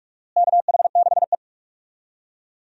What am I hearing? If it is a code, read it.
Morse code, heard as a keyed tone, about 46 wpm, 700 Hz. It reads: KH6E
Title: DXCC-100 Call Signs - 4 Characters - Repeated 2x - 45wpm